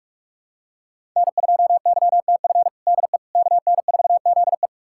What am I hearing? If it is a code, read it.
Morse code, heard as a keyed tone, about 46 wpm, 700 Hz. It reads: N2YTF DE KN4ZE
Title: USA Callsign DE Callsign - 5 Characters - 45wpm